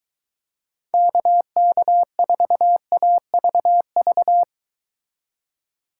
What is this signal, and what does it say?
Morse code, heard as a keyed tone, about 22 wpm, 700 Hz. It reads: KK4AVV